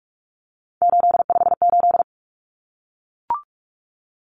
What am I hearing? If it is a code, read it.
Morse code, heard as a keyed tone, about 44 wpm, 700 Hz. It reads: 858
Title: Numbers - 3-Digit 45wpm